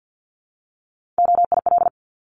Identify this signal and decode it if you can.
Morse code, heard as a keyed tone, about 50 wpm, 690 Hz. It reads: OIL
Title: Multi-letter Phonemes Repeated 3x 50wpm